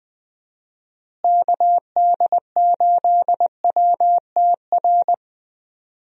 Morse code, heard as a keyed tone, about 20 wpm, 700 Hz. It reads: KD8WTR
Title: USA Call Signs - 6 Characters - Repeated 3x - 20wpm